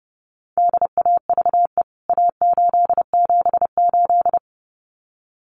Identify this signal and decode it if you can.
Morse code, heard as a keyed tone, about 30 wpm, 700 Hz. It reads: DAVE A878
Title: CWT Exchange 30wpm